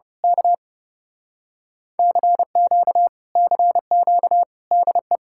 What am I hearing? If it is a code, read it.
Morse code, heard as a keyed tone, about 29 wpm, 700 Hz. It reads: K CQ CQ DE
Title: Speed-Racing Common QSO Phrases Encore - 30wpm